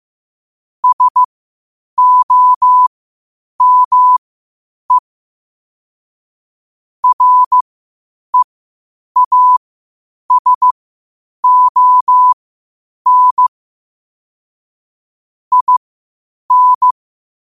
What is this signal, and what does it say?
Morse code, heard as a keyed tone, about 15 wpm, spaced as for 8 wpm, 1000 Hz. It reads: SOME REASON IN
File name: qotdfiles / 8wpm